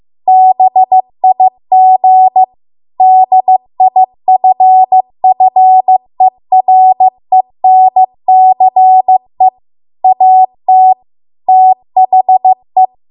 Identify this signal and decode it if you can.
Morse code, heard as a keyed tone, about 15 wpm, 750 Hz. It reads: BIG DIFFERENCE AT THE